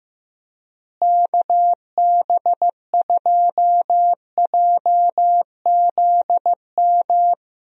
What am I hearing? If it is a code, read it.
Morse code, heard as a keyed tone, about 15 wpm, 700 Hz. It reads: KB2JZM
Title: USA Call Signs - 6 Characters - Repeated 3x - 15wpm